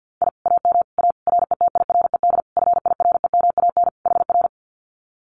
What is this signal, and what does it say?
Morse code, heard as a keyed tone, about 50 wpm, 700 Hz. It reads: I AM A RETIRED FIREMAN HR
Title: Common QSO Phrases Encore 50wpm